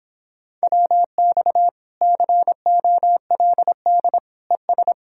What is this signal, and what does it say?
Morse code, heard as a keyed tone, about 26 wpm, 700 Hz. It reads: WX COLD ES